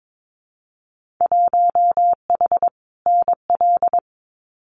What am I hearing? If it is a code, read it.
Morse code, heard as a keyed tone, about 22 wpm, 700 Hz. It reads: 1H NL
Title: Winter Field Day Exchanges 22wpm